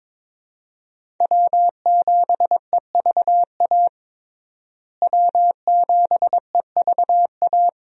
Morse code, heard as a keyed tone, about 22 wpm, 700 Hz. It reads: W7EVA W7EVA
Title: USA Call Signs - 5 Characters - Repeated 3x - 22wpm